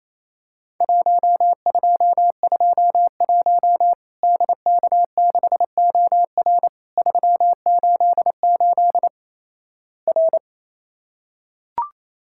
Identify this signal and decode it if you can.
Morse code, heard as a keyed tone, about 28 wpm, 700 Hz. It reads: 1221 DK6OR 388 R
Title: Worked All Europe DX Contest - QTC Exchange - 28wpm